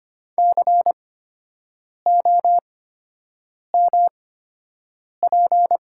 Morse code, heard as a keyed tone, about 25 wpm, spaced as for 7 wpm, 700 Hz. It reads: COMP
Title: 8x Character Spacing - Top 500 Words - 25wpm